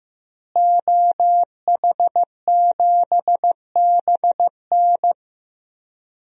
Morse code, heard as a keyed tone, about 15 wpm, 700 Hz. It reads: OH7BN
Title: DXCC-100 Call Signs - 5 Characters - Repeated 2x - 15wpm